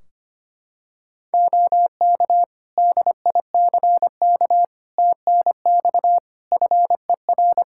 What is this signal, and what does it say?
Morse code, heard as a keyed tone, about 25 wpm, 700 Hz. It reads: OK DICK TNX FER
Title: Practice QSO - Exchange 4 - Ending - 25wpm